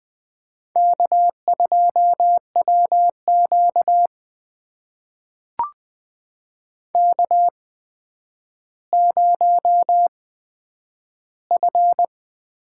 Morse code, heard as a keyed tone, about 20 wpm, spaced as for 10 wpm, 700 Hz. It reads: K2WQ K0F
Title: USA Call Signs - 8x Character Spacing - 20wpm